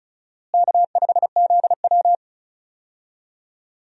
Morse code, heard as a keyed tone, about 34 wpm, 700 Hz. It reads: K5ZW
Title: USA Call Signs - 4 Characters - Repeated 3x - 35wpm